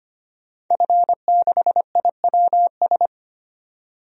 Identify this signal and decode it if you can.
Morse code, heard as a keyed tone, about 25 wpm, 700 Hz. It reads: F6IWS
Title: DXCC-100 Call Signs - 5 Characters - 25wpm